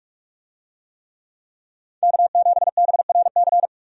Morse code, heard as a keyed tone, about 46 wpm, 700 Hz. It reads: K7BRC
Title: USA Call Signs - 5 Characters - 45wpm